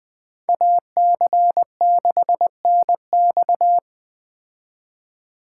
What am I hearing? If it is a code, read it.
Morse code, heard as a keyed tone, about 21 wpm, 700 Hz. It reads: AC6NX
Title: Speed Racing - USA 2x2 Call Signs 17wpm